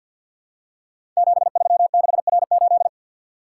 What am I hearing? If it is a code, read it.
Morse code, heard as a keyed tone, about 48 wpm, 700 Hz. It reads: 736R8
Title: Code Groups - 5 Letter:Numbers - 50wpm